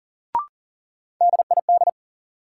Morse code, heard as a keyed tone, about 40 wpm, 700 Hz. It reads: DID
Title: Sentences from Top 100 Words - New Word Drill - 40wpm